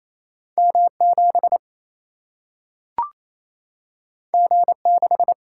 Morse code, heard as a keyed tone, about 28 wpm, 700 Hz. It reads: M7 G6
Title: Single-Letter Single-Number 28wpm